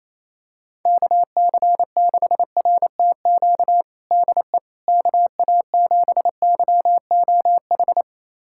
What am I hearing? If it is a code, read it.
Morse code, heard as a keyed tone, about 28 wpm, 700 Hz. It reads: KC6RTQ DE KA7YOH
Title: USA Callsign DE Callsign - 6 Characters - 28wpm